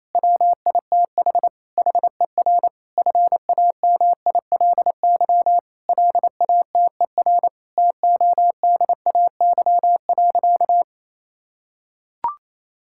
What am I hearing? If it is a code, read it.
Morse code, heard as a keyed tone, about 28 wpm, 700 Hz. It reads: WITH HER FAMILY LATER TODAY.
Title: Sentences from Top 1600 Words 28wpm